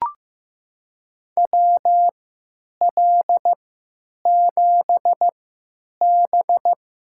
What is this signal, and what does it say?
Morse code, heard as a keyed tone, about 15 wpm, spaced as for 9 wpm, 700 Hz. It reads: WL7B
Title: USA Call Signs - 3x Character Spacing - 15wpm